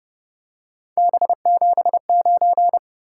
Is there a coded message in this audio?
Yes: Morse code from 1.0 to 2.8 s, heard as a keyed tone, about 31 wpm, 700 Hz. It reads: B79